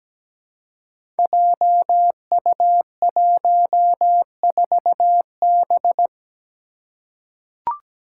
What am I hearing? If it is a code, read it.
Morse code, heard as a keyed tone, about 17 wpm, 700 Hz. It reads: JU14B
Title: Five Challenging Letter-Number Combinations 17wpm